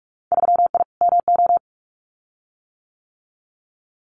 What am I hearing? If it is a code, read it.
Morse code, heard as a keyed tone, about 46 wpm, 700 Hz. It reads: 3I MO
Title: Winter Field Day Exchanges 45wpm